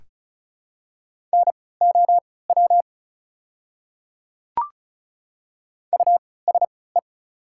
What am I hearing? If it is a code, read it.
Morse code, heard as a keyed tone, about 36 wpm, spaced as for 20 wpm, 700 Hz. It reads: NOW USE
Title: 3x Character Spacing - Top 200 Words - 35wpm